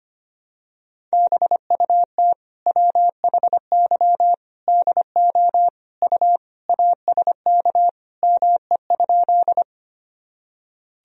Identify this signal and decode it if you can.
Morse code, heard as a keyed tone, about 25 wpm, 700 Hz. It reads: BUT WHY DO U ASK ME?
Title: Sentences from Top 200 Words and Abbreviations 25wpm